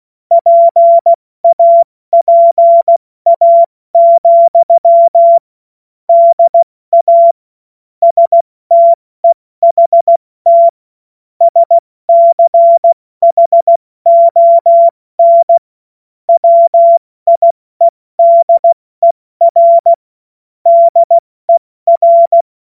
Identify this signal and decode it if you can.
Morse code, heard as a keyed tone, about 16 wpm, 680 Hz. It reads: PAPA, DA STEHT SCHON WIEDER DER
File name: Morsekurs / Witze / Witz 25. April 70 F80